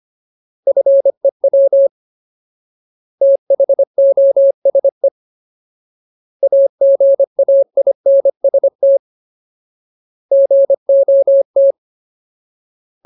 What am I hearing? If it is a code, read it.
Morse code, heard as a keyed tone, about 25 wpm, 550 Hz. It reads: FEW THOSE AGAINST GOT
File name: Ultimate Word List / 25WPM Ultimate